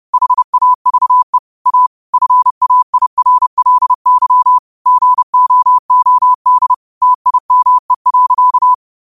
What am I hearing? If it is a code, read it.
Morse code, heard as a keyed tone, about 30 wpm, 1000 Hz. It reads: HAVE A FAIRLY GOOD TIME.
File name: qotdfiles / 30wpm